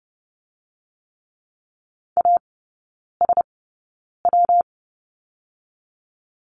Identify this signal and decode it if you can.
Morse code, heard as a keyed tone, about 30 wpm, 700 Hz. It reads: A S W
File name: Lesson 19e - Letter w - Three Characters 30wpm